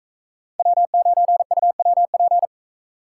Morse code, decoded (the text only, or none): W9UWP